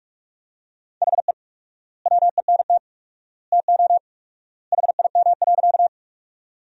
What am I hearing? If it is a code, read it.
Morse code, heard as a keyed tone, about 43 wpm, 700 Hz. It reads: HE WENT TO HIM.
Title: 4x Word-Spacing - Sentences from Top 200 Words - 45wpm